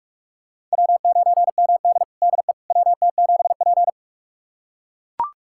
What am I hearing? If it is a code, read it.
Morse code, heard as a keyed tone, about 42 wpm, 700 Hz. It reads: W9MD DE WT7P
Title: USA Callsign DE Callsign - 4 Characters - 45wpm